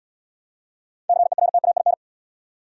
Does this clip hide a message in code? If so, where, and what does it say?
Yes: Morse code from 1.1 to 1.9 s, heard as a keyed tone, about 71 wpm, 700 Hz. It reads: CERTAIN